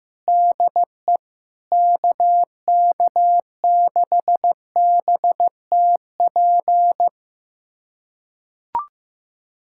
Morse code, heard as a keyed tone, about 16 wpm, 700 Hz. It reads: DE KK6BTP
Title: USA Callsign DE Callsign 15wpm